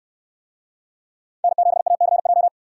Morse code, heard as a keyed tone, about 68 wpm, 700 Hz. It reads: N7ACJ